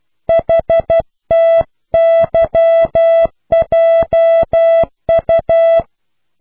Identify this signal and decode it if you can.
Morse code, heard as a keyed tone, about 12 wpm, 660 Hz. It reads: HTYJU